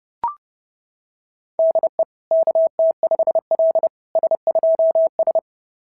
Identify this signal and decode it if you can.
Morse code, heard as a keyed tone, about 30 wpm, 660 Hz. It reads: DE KT5L S2S
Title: SOTA QSO Practice 30wpm